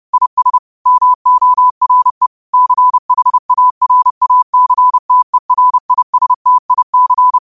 Morse code, heard as a keyed tone, about 30 wpm, 1000 Hz. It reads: IS MORE CHARACTERISTIC
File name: qotdfiles / 30wpm